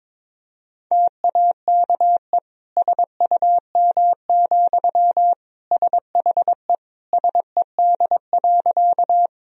Morse code, heard as a keyed tone, about 22 wpm, 700 Hz. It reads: TAKE SUM, SHE SED.
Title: Sentences from Top 200 Words and Abbreviations 22wpm